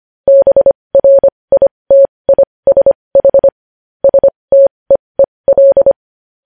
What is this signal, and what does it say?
Morse code, heard as a keyed tone, about 25 wpm, spaced as for 20 wpm, 550 Hz. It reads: BRITISH STEEL
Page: News Headlines in Morse Code at 20 WPM